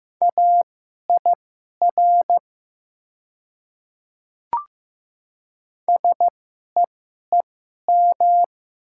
Morse code, heard as a keyed tone, about 15 wpm, spaced as for 11 wpm, 700 Hz. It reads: AIR SEEM
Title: 2x Character Spacing - Top 300 Words - 15wpm